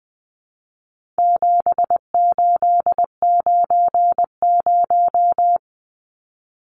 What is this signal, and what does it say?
Morse code, heard as a keyed tone, about 20 wpm, 700 Hz. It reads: 7890
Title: Numbers - 4-Digit 20wpm